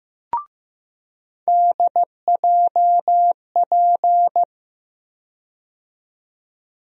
Morse code, heard as a keyed tone, about 15 wpm, 700 Hz. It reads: DJP
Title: USA Call Sign Postfixes - 3 Letters 15wpm